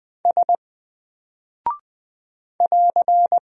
Morse code, heard as a keyed tone, about 20 wpm, 700 Hz. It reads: S <AR>
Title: Single Character (Letter:Number:Punctuation:Prosign) Alt 20wpm